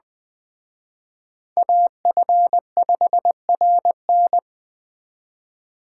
Morse code, heard as a keyed tone, about 20 wpm, 700 Hz. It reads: AF5RN
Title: Speed Racing - USA 2x2 Call Signs 17wpm